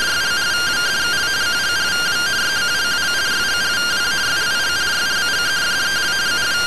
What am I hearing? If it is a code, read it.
Two-tone teleprinter signal, about 170 Hz shift.
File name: findings / MR AXIAL 33,99mm 1.gen-13 rtty rx